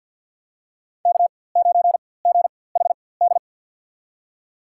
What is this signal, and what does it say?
Morse code, heard as a keyed tone, about 49 wpm, spaced as for 23 wpm, 700 Hz. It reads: K9GHD
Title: USA Call Signs - 4x Character Spacing - 50wpm